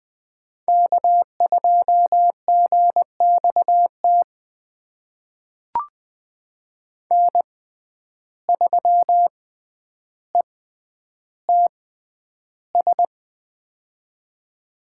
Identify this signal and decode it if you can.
Morse code, heard as a keyed tone, about 20 wpm, spaced as for 10 wpm, 700 Hz. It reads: K2GXT N3ETS
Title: USA Call Signs - 5 Characters - 6x Character Spacing - 20wpm